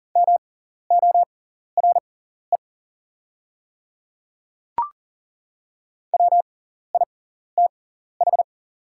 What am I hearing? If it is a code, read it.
Morse code, heard as a keyed tone, about 41 wpm, spaced as for 14 wpm, 700 Hz. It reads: MORE WITH
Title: 6x Character Spacing - Top 200 Words - 40wpm